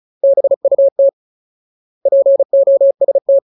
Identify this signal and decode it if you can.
Morse code, heard as a keyed tone, about 34 wpm, 550 Hz. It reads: BUT POST
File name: Ultimate Word List / 35WPM Ultimate